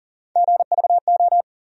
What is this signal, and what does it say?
Morse code, heard as a keyed tone, about 40 wpm, 700 Hz. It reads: GVO
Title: USA Call Sign Postfixes - 3 Letters 40wpm